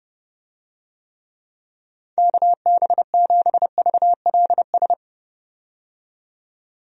Morse code, heard as a keyed tone, about 30 wpm, 700 Hz. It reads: KB7VLS
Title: USA Call Signs - 6 Characters - 30wpm